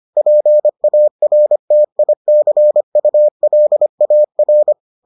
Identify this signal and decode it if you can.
Morse code, heard as a keyed tone, about 25 wpm, 600 Hz. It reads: PARTICULAR